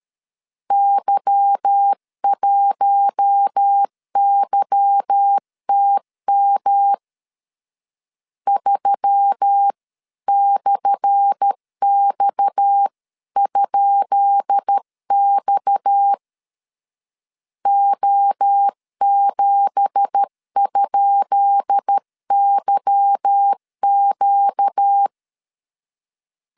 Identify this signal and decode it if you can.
Morse code, heard as a keyed tone, about 13 wpm, 800 Hz. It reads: Y1YTM 3/X?X O7?YQ
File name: LEZIONI / 10 SIGNS / 10B INT 12wpm 2414